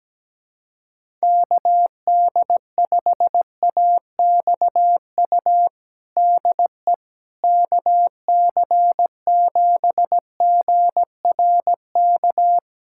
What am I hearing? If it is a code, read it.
Morse code, heard as a keyed tone, about 17 wpm, 700 Hz. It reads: KD5AXU DE KC7GRK
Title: USA Callsign DE Callsign - 6 Characters - 17wpm